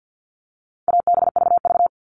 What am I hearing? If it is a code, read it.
Morse code, heard as a keyed tone, about 50 wpm, 700 Hz. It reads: ABVV
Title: All Cut Numbers - 4-Digit 50wpm